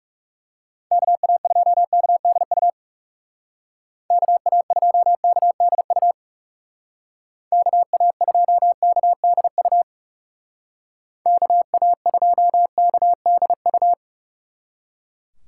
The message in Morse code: KA2KDU KA2KDU KA2KDU KA2KDU

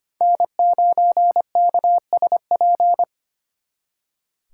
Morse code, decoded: N9KSP

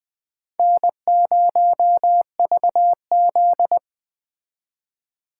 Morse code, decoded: N0VZ

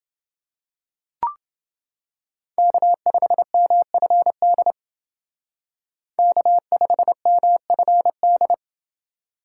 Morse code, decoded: K5MFD K5MFD

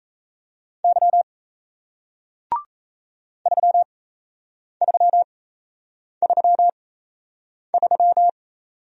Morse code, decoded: Y 3333

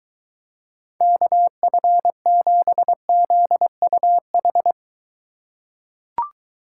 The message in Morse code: KF7ZUH

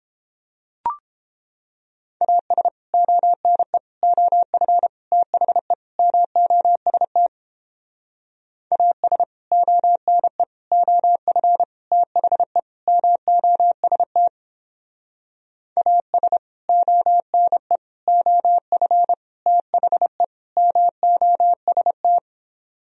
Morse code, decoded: AS ONE OF THE MOST AS ONE OF THE MOST AS ONE OF THE MOST

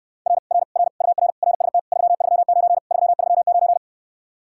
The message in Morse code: R R R UR RST 339 339

